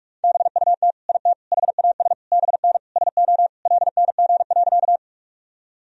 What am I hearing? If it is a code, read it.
Morse code, heard as a keyed tone, about 41 wpm, 700 Hz. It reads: BUT IT HAS BN SO LNG.
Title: Sentences from Top 100 Words and Abbreviations 45wpm